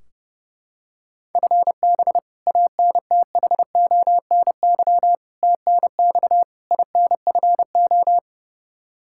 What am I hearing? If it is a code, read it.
Morse code, heard as a keyed tone, about 30 wpm, 700 Hz. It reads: FB ANTHONY TNX INFO
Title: K1USN SST QSO Practice 30wpm